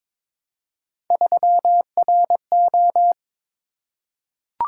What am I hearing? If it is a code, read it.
Morse code, heard as a keyed tone, about 22 wpm, 700 Hz. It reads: 3RO
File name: practice / 22wpm / Three Letter-Number Combinations - 22wpm